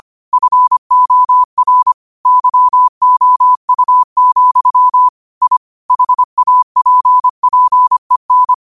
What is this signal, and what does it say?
Morse code, heard as a keyed tone, about 25 wpm, 1000 Hz. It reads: FOR YOU, I HAPPEN